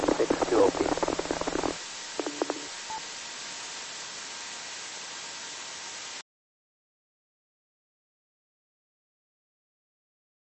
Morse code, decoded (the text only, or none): none